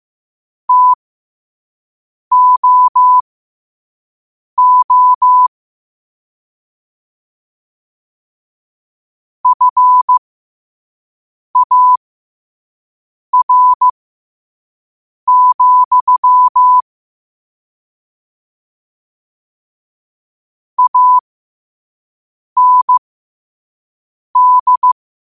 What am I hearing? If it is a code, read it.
Morse code, heard as a keyed tone, about 15 wpm, spaced as for 5 wpm, 1000 Hz. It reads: TOO FAR, AND
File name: qotdfiles / 5wpm